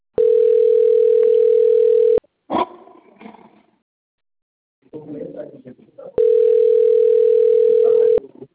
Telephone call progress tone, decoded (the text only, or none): ringback tone